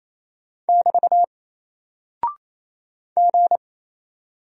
Morse code, heard as a keyed tone, about 28 wpm, 700 Hz. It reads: <BT> G
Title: Single Character (Letter:Number:Punctuation:Prosign) Alt 28wpm